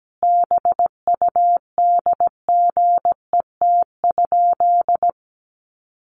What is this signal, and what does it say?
Morse code, heard as a keyed tone, about 17 wpm, 700 Hz. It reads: BUDGET?
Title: Sentences from Top 1200 Words 17wpm